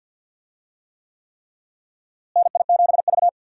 Morse code, heard as a keyed tone, about 52 wpm, 700 Hz. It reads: NI7V
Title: USA Call Signs - 4 Characters - Repeated 3x - 50wpm